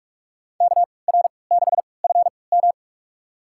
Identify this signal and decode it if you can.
Morse code, heard as a keyed tone, about 46 wpm, spaced as for 26 wpm, 700 Hz. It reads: KR6FM
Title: USA Call Signs - 5 Characters - 3x Character Spacing - 45wpm